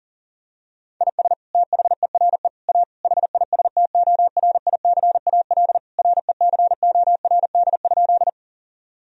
Morse code, heard as a keyed tone, about 40 wpm, 700 Hz. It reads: IS THERE A HISTORICAL RECORD?
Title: Sentences from Top 1000 Words - Review Part 4 - 40wpm